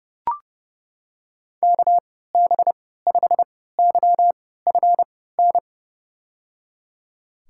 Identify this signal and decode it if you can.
Morse code, heard as a keyed tone, about 30 wpm, spaced as for 17 wpm, 700 Hz. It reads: KB5YFN
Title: USA Call Signs - 6 Characters - 3x Character Spacing - 30wpm